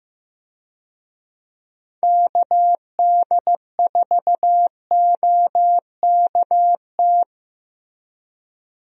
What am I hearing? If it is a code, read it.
Morse code, heard as a keyed tone, about 15 wpm, 700 Hz. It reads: KD4OKT
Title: USA Call Signs - 6 Characters - Repeated 3x - 15wpm